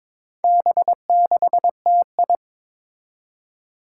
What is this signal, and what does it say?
Morse code, heard as a keyed tone, about 22 wpm, 700 Hz. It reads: B6TI
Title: Four Challenging Letter-Number Combinations 22wpm